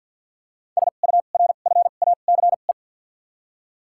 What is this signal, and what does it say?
Morse code, heard as a keyed tone, about 53 wpm, spaced as for 37 wpm, 700 Hz. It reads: SURFACE
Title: 2x Character Spacing - Top 500 Words - 50wpm